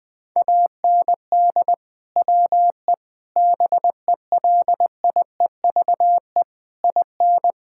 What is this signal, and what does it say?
Morse code, heard as a keyed tone, about 21 wpm, 700 Hz. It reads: AND WE BELIEVE IN
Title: Sentences from Top 1600 Words 20wpm